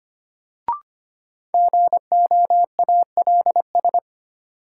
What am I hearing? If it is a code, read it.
Morse code, heard as a keyed tone, about 25 wpm, 700 Hz. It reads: GOALS
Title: Sentences from Top 1000 Words - Review Part 6 - New Word Drill - 25wpm